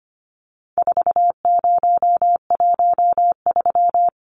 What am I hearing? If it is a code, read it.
Morse code, heard as a keyed tone, about 25 wpm, 700 Hz. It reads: 4013
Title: Numbers - 4-Digit 25wpm